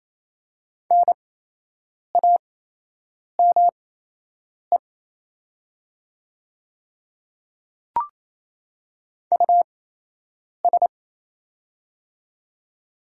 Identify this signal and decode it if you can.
Morse code, heard as a keyed tone, about 27 wpm, spaced as for 8 wpm, 700 Hz. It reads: NAME US
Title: 8x Character Spacing - Top 200 Words - 28wpm